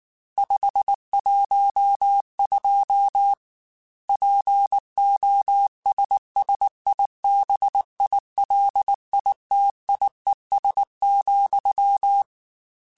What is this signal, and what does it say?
Morse code, heard as a keyed tone, about 19 wpm, 800 Hz. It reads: 512 POSSIBILITIES,
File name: Technical Subjects / Technical Subjects-mp3s / AND Gate @18